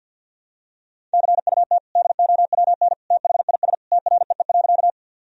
Morse code, heard as a keyed tone, about 49 wpm, 700 Hz. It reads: CUT DOWN THIS TREE.
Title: Sentences from Top 1000 Words - Review Part 1 - 50wpm